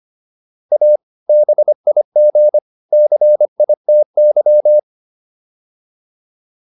Morse code, heard as a keyed tone, about 25 wpm, 600 Hz. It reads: A BIG CITY